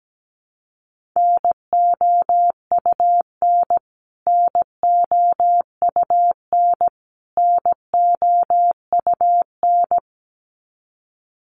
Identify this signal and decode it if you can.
Morse code, heard as a keyed tone, about 17 wpm, 700 Hz. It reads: NOUN NOUN NOUN
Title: Top 500 Words Repeated 17wpm